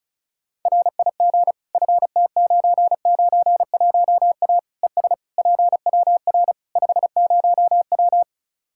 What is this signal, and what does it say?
Morse code, heard as a keyed tone, about 34 wpm, 700 Hz. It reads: RIG FT991A ES PWR 50W
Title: Practice QSO - Exchange 2 - Info - 35wpm